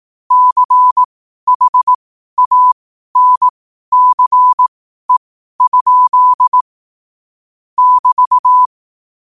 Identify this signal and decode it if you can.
Morse code, heard as a keyed tone, about 18 wpm, spaced as for 12 wpm, 1000 Hz. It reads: CHANCE? <BT>
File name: qotdfiles / 12wpm